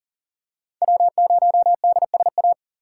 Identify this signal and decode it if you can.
Morse code, heard as a keyed tone, about 40 wpm, 700 Hz. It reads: W0DSA
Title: USA Call Signs - 3x Character Spacing - 40wpm